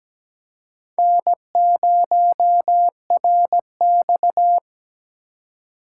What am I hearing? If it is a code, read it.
Morse code, heard as a keyed tone, about 20 wpm, 700 Hz. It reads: N0RX
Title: Speed Racing - USA 1x2 Call Signs 17wpm